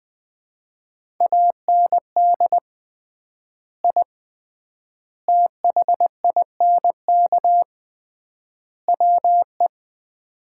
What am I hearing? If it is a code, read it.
Morse code, heard as a keyed tone, about 20 wpm, 700 Hz. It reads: AND I THINK WE
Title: 3x Word-Spacing - Sets of 4 Words - 20wpm